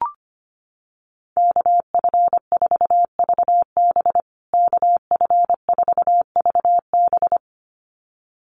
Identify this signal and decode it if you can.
Morse code, heard as a keyed tone, about 25 wpm, 700 Hz. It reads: KF4VB KF4VB
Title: USA Call Signs Repeated 2x 25wpm